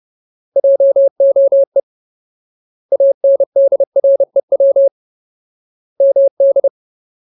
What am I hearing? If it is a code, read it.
Morse code, heard as a keyed tone, about 30 wpm, 550 Hz. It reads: JOE ANDREW MD
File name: Ultimate Word List / 30WPM Ultimate